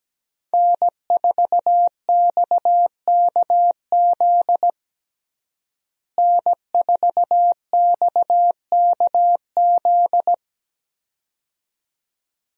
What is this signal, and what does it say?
Morse code, heard as a keyed tone, about 17 wpm, 700 Hz. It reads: N4XKZ N4XKZ
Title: USA Call Signs - 5 Characters - Repeated 2x - 17wpm